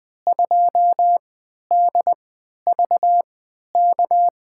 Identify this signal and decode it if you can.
Morse code, heard as a keyed tone, about 20 wpm, spaced as for 11 wpm, 700 Hz. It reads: 2DVK
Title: USA Call Signs - 6 Characters - 3x Character Spacing - 20wpm